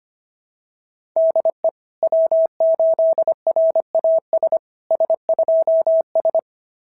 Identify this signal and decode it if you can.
Morse code, heard as a keyed tone, about 25 wpm, 690 Hz. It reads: DE W8RAS S2S
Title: SOTA QSO Practice 25wpm